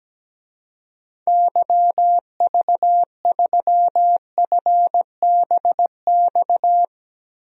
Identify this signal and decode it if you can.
Morse code, heard as a keyed tone, about 17 wpm, 700 Hz. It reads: YV3FBX
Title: DXCC-100 Call Signs - 6 Characters - Repeated 3x - 17wpm